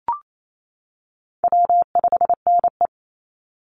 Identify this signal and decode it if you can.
Morse code, heard as a keyed tone, about 28 wpm, 700 Hz. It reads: W5NE
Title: Easy Call Signs - 28wpm